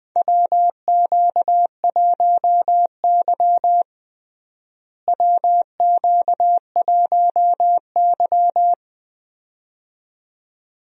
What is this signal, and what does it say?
Morse code, heard as a keyed tone, about 20 wpm, 700 Hz. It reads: WQ1Y WQ1Y
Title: USA Call Signs - 4 Characters - Repeated 2x - 20wpm